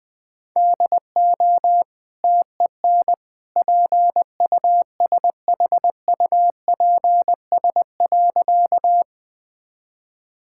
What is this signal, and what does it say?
Morse code, heard as a keyed tone, about 20 wpm, 700 Hz. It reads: DO TEN PUSHUPS.
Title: Easy Sentences with Hints - 20wpm